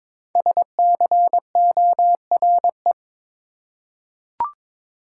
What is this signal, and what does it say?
Morse code, heard as a keyed tone, about 22 wpm, 700 Hz. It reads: SCORE
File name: 5 Letter Words - First 3 Letters Given - 22wpm